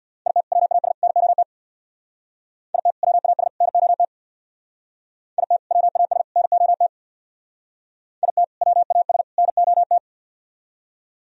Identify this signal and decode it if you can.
Morse code, heard as a keyed tone, about 62 wpm, 700 Hz. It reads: IT WAS NOT IT WAS NOT IT WAS NOT IT WAS NOT
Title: Speed-Racing - Sets of 3 Words 50wpm